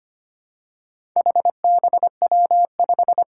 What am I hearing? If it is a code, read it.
Morse code, heard as a keyed tone, about 25 wpm, 700 Hz. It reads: HBW5
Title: Four Challenging Letter-Number Combinations 25wpm